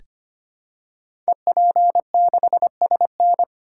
Morse code, heard as a keyed tone, about 25 wpm, 700 Hz. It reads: EP6SN
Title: Code Groups - 5 Letter:Numbers - 25wpm